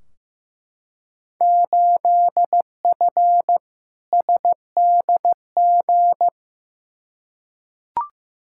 Morse code, heard as a keyed tone, about 15 wpm, 700 Hz. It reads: 8F SDG
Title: Field Day Exchanges Encore - 15wpm